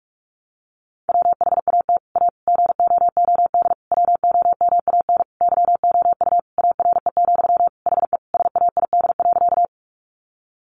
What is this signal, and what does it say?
Morse code, heard as a keyed tone, about 44 wpm, 700 Hz. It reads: WHAT A GOOD WOMAN YOU ARE, HE SAID.
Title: Sentences from Top 1100 Words 45wpm